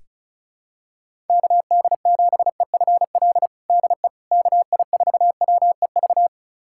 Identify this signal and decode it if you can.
Morse code, heard as a keyed tone, about 36 wpm, 700 Hz. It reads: KD7EFL DE KI4WEV
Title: USA Callsign DE Callsign - 6 Characters - 35wpm